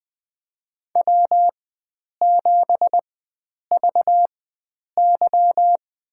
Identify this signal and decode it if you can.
Morse code, heard as a keyed tone, about 21 wpm, spaced as for 9 wpm, 700 Hz. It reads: W7VY
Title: USA Call Signs - 4 Characters - 4x Character Spacing - 20wpm